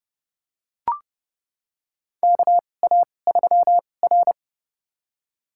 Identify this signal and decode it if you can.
Morse code, heard as a keyed tone, about 32 wpm, spaced as for 22 wpm, 700 Hz. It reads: KA3R
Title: USA Call Signs - 4 Characters - 2x Character Spacing - 30wpm